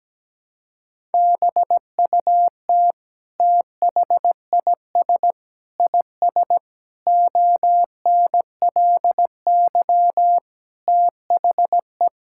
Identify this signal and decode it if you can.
Morse code, heard as a keyed tone, about 17 wpm, 700 Hz. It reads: BUT THIS IS ONLY THE
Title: Sentences from Top 1000 Words - Review Part 5 - 17wpm